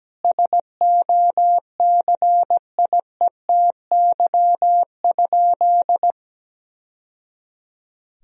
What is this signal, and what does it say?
Morse code, heard as a keyed tone, about 17 wpm, 700 Hz. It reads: SOCIETY?
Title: Sentences from Top 1000 Words - Review Part 4 - 17wpm